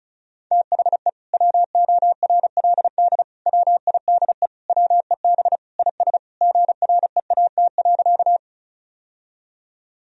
Morse code, heard as a keyed tone, about 34 wpm, 700 Hz. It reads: THE WORLD WIDE WEB IS GREAT.
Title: Sentences from Top 1000 Words - Review Part 4 - 35wpm